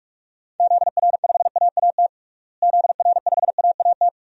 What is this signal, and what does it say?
Morse code, heard as a keyed tone, about 41 wpm, 700 Hz. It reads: ZR5AAT ZR5AAT
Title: DXCC-100 Call Signs - 6 Characters - Repeated 2x - 45wpm